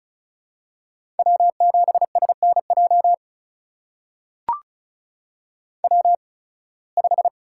Morse code, heard as a keyed tone, about 36 wpm, spaced as for 23 wpm, 700 Hz. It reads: W7SNJ W5
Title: USA Call Signs - 5 Characters - 8x Character Spacing - 35wpm